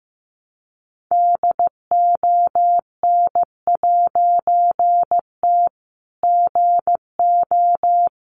Morse code, heard as a keyed tone, about 15 wpm, 700 Hz. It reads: DON'T GO
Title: A Sound Of Thunder 15wpm